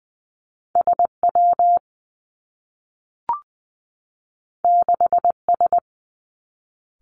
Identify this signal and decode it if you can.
Morse code, heard as a keyed tone, about 20 wpm, 700 Hz. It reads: S W 6 S
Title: Two Challenging Letter-Number Combinations 20wpm